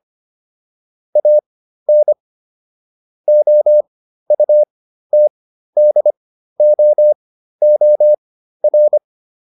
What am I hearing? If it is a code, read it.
Morse code, heard as a keyed tone, about 25 wpm, spaced as for 13 wpm, 600 Hz. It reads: AN OUTDOOR